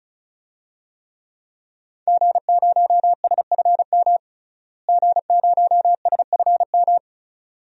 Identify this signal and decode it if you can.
Morse code, heard as a keyed tone, about 34 wpm, 700 Hz. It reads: G0SFM G0SFM
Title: DXCC-100 Call Signs - 5 Characters - Repeated 2x - 35wpm